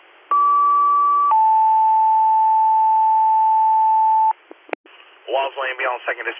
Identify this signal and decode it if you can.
Two-tone page, heard as a one-second tone, then a three-second tone, about 1150 then 880 Hz, with speech. It is heard from 0.3 to 4.3 s.